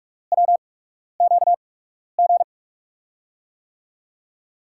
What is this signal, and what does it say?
Morse code, heard as a keyed tone, about 44 wpm, spaced as for 12 wpm, 700 Hz. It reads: WQG
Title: USA Call Signs - 6 Characters - 8x Character Spacing - 45wpm